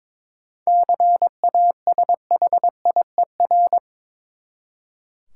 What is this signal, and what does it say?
Morse code, heard as a keyed tone, about 22 wpm, 700 Hz. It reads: CASHIER
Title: Practice QSO - Exchange 3 - More Info - 22wpm